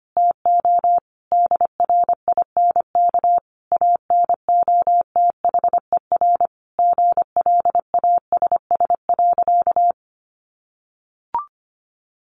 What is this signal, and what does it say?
Morse code, heard as a keyed tone, about 25 wpm, 700 Hz. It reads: TO DRINK ANOTHER GLASS.
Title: Sentences from Top 1500 Words 25wpm